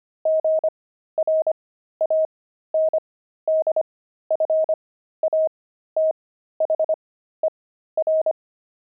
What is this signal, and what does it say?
Morse code, heard as a keyed tone, about 25 wpm, spaced as for 13 wpm, 630 Hz. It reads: GRANDFATHER
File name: pangram-fs-13-cs-25-grandpa